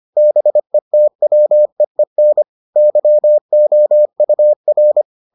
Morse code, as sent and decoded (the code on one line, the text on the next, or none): -... - .-- . -. / -.-- --- ..- .-.
BETWEEN YOUR